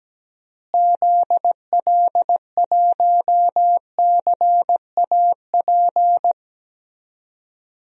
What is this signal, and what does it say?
Morse code, heard as a keyed tone, about 17 wpm, 700 Hz. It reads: ZL1CAP
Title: DXCC-100 Call Signs - 6 Characters - Repeated 2x - 17wpm